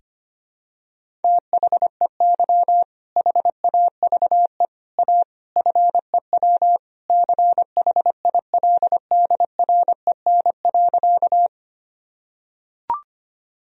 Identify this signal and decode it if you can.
Morse code, heard as a keyed tone, about 25 wpm, 700 Hz. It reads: THEY HAVE A FEW CHILDREN.
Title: Sentences from Top 1000 Words - Review Part 1 - 25wpm